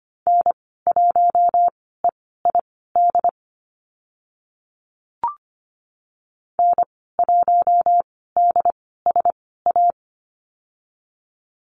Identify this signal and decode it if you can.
Morse code, heard as a keyed tone, about 25 wpm, spaced as for 16 wpm, 690 Hz. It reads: N1EID N1DSA
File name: Lesson 15g - Number 5 - Call Signs 25wpm